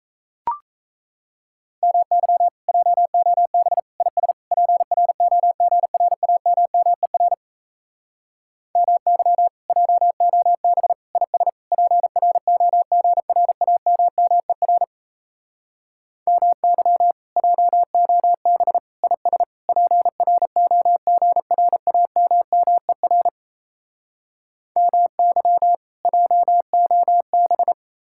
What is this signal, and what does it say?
Morse code, heard as a keyed tone, about 37 wpm, 700 Hz. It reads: MY JOB IS PROGRAMMER MY JOB IS PROGRAMMER MY JOB IS PROGRAMMER MY JOB